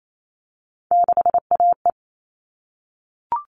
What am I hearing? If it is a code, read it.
Morse code, heard as a keyed tone, about 28 wpm, 700 Hz. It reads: 6AE
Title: All Cut Numbers - 3-Digit 28wpm